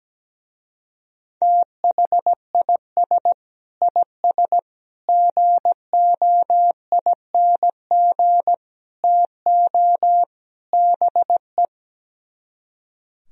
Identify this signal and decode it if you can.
Morse code, heard as a keyed tone, about 17 wpm, 700 Hz. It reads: THIS IS GOING TO BE
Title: Speed-Racing - Sets of 5 Words 17wpm